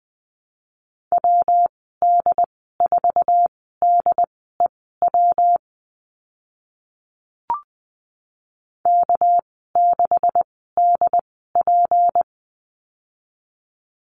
Morse code, heard as a keyed tone, about 20 wpm, spaced as for 14 wpm, 700 Hz. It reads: WD4DEW K6DP
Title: 219 Lesson 35g - Number 0 - Call Signs 20wpm